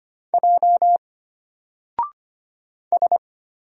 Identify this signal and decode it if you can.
Morse code, heard as a keyed tone, about 25 wpm, 700 Hz. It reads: J S
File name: Single-Letter-Number-25wpm